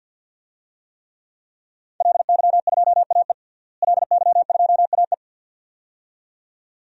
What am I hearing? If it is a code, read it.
Morse code, heard as a keyed tone, about 52 wpm, 700 Hz. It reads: LY2AE LY2AE
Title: DXCC-100 Call Signs - 5 Characters - Repeated 2x - 50wpm